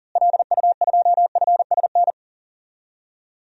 Morse code, heard as a keyed tone, about 40 wpm, 700 Hz. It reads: LU2FSN
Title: DXCC-100 Call Signs - 6 Characters - Repeated 3x - 40wpm